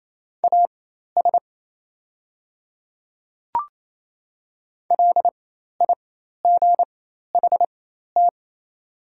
Morse code, heard as a keyed tone, about 28 wpm, spaced as for 13 wpm, 700 Hz. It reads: AS LIGHT
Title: 4x Character Spacing - Top 500 Words - 28wpm